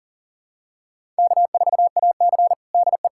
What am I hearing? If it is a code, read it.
Morse code, heard as a keyed tone, about 37 wpm, 700 Hz. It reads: K4AC DE
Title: USA Callsign DE Callsign - 4 Characters - 40wpm